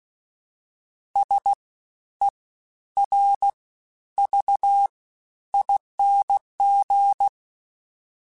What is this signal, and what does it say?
Morse code, heard as a keyed tone, about 16 wpm, 800 Hz. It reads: S E R V ING
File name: Special INGs-mp3s / Special INGs-mp3s / SlowToFaster ing @15